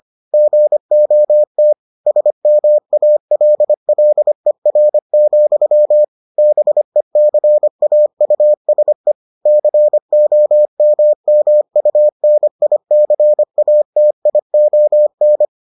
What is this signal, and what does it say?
Morse code, heard as a keyed tone, about 25 wpm, 600 Hz. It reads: GOT SMALLER, BECAUSE COMMUNICATION